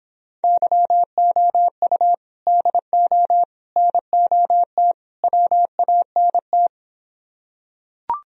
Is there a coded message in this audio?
Yes: Morse code from 0.4 to 6.7 s, heard as a keyed tone, about 26 wpm, 700 Hz. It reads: YOU DO NOT WANT